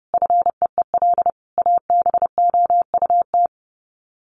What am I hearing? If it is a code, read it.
Morse code, heard as a keyed tone, about 30 wpm, 700 Hz. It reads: FEEL ABOUT
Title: Sets of 5 Words 30wpm